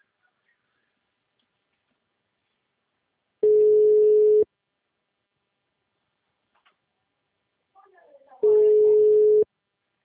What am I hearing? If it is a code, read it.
Ringback tone (Europe, 425 Hz), with speech.